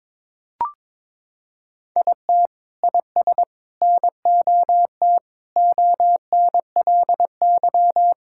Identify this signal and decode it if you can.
Morse code, heard as a keyed tone, about 22 wpm, 700 Hz. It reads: IT IS NOT ONLY
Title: Sentences from Top 1600 Words 22wpm